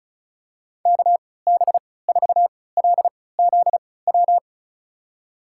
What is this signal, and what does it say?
Morse code, heard as a keyed tone, about 36 wpm, spaced as for 20 wpm, 700 Hz. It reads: KB4LZW
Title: USA Call Signs - 6 Characters - 3x Character Spacing - 35wpm